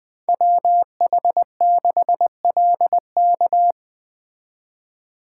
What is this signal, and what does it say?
Morse code, heard as a keyed tone, about 24 wpm, 700 Hz. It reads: WH6LK
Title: Speed Racing - USA 2x2 Call Signs 20wpm